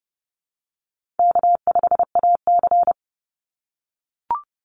Morse code, heard as a keyed tone, about 30 wpm, 700 Hz. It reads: K5AC
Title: USA Call Signs 30wpm